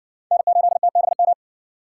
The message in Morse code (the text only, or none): N8TBM